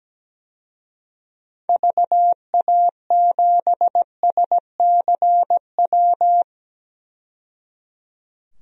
Morse code, heard as a keyed tone, about 17 wpm, 700 Hz. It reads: VA7SCW
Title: DXCC-100 Call Signs - 6 Characters - 17wpm